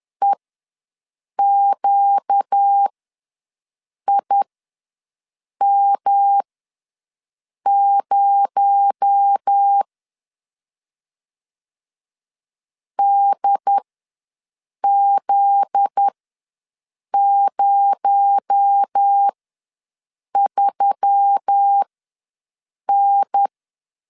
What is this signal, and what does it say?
Morse code, heard as a keyed tone, about 10 wpm, spaced as for 6 wpm, 800 Hz. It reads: EQIM0 DZ03N
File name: LEZIONI / 7 GF9Z / 7 GF9Z ws 10wpm 3919